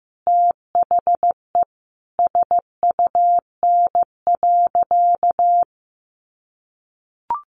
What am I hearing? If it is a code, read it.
Morse code, heard as a keyed tone, about 15 wpm, 700 Hz. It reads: THE SUN.
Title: Sentences from Top 500 Words 15wpm